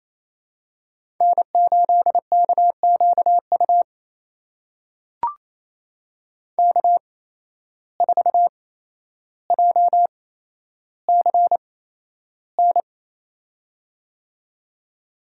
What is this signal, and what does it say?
Morse code, heard as a keyed tone, about 27 wpm, spaced as for 12 wpm, 700 Hz. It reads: N8KQU K4JCN
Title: USA Call Signs - 5 Characters - 8x Character Spacing - 28wpm